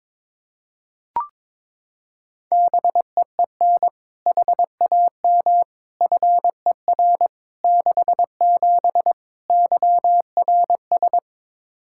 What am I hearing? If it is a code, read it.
Morse code, heard as a keyed tone, about 21 wpm, 700 Hz. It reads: BEEN HAM FER 67 YRS
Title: Practice QSO - Exchange 3 - More Info - 22wpm